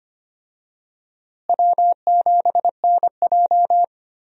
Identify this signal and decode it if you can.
Morse code, heard as a keyed tone, about 25 wpm, 700 Hz. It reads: W7NJ
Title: USA Call Signs - 4 Characters - Repeated 2x - 25wpm